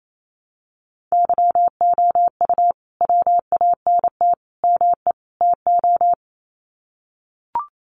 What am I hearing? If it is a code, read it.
Morse code, heard as a keyed tone, about 28 wpm, 700 Hz. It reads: YOU WANT ME TO
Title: Sets of 4 Words 28wpm